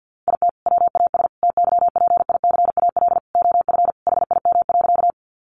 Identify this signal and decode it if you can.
Morse code, heard as a keyed tone, about 50 wpm, 700 Hz. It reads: IT WAS TYPICAL OF HIM.
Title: Sentences from Top 1100 Words 50wpm